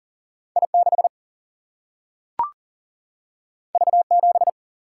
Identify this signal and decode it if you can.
Morse code, heard as a keyed tone, about 40 wpm, 700 Hz. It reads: I 6 V 7
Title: Single-Letter Single-Number 40wpm